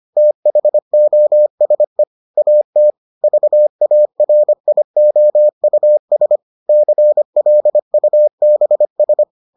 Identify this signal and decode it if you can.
Morse code, heard as a keyed tone, about 25 wpm, 600 Hz. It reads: THOSE AT VARIOUS CLUBS